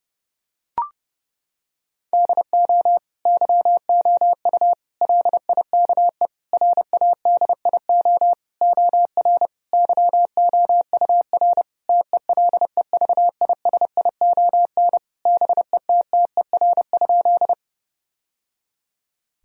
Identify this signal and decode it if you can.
Morse code, heard as a keyed tone, about 32 wpm, 700 Hz. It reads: DO YOU LIKE RADIO OR YOUR TELEVISION BETTER?
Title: Sentences from Top 1000 Words - Review Part 4 - 30wpm